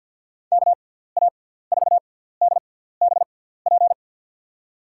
Morse code, heard as a keyed tone, about 50 wpm, spaced as for 17 wpm, 700 Hz. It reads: KA4DBP
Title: USA Call Signs - 6x Character Spacing - 50wpm